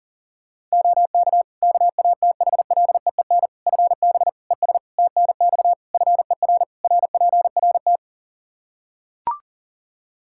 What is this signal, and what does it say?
Morse code, heard as a keyed tone, about 40 wpm, 700 Hz. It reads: OK KATHLEEN FB ES TNX FER RPRT
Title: Practice QSO - Exchange 2 - Info - 40wpm